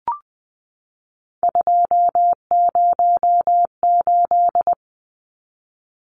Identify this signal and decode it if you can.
Morse code, heard as a keyed tone, about 20 wpm, 700 Hz. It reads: 208
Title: Numbers - 3-Digit 20wpm